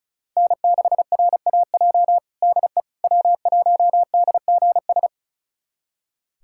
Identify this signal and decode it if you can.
Morse code, heard as a keyed tone, about 36 wpm, 700 Hz. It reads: N6RAJ DE W1DGS
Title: USA Callsign DE Callsign - 5 Characters - 35wpm